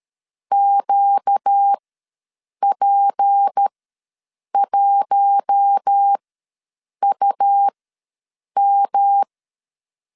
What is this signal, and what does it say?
Morse code, heard as a keyed tone, about 13 wpm, spaced as for 7 wpm, 800 Hz. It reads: QP1UM
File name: LEZIONI / 6 PKX0 / 6 P ws 12wpm 3117